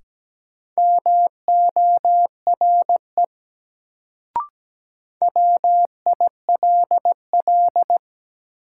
Morse code, heard as a keyed tone, about 17 wpm, 700 Hz. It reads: MORE WILL
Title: Sentences from Top 100 Words - New Word Drill - 17wpm